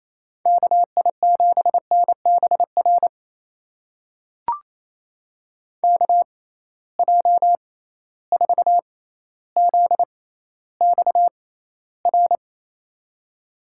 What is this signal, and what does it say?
Morse code, heard as a keyed tone, about 28 wpm, spaced as for 14 wpm, 700 Hz. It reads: KI7NBR KJ4ZXR
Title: USA Call Signs - 6x Character Spacing - 28wpm